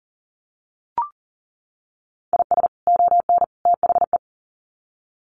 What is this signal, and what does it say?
Morse code, heard as a keyed tone, about 40 wpm, 700 Hz. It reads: IS ON THE
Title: Sets of 3 Words 40wpm